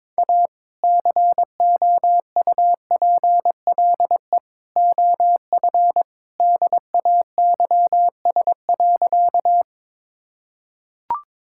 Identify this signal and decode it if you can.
Morse code, heard as a keyed tone, about 22 wpm, 700 Hz. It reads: A COUPLE OF DAYS.
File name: practice / 22wpm / Sentences with 2-Word Phrases - 22wpm